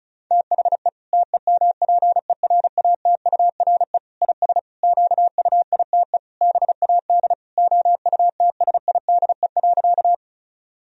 Morse code, heard as a keyed tone, about 34 wpm, 700 Hz. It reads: THE TEMPERATURE IS QUITE BAD OUTSIDE.
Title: Sentences from Top 1000 Words - Review Part 4 - 35wpm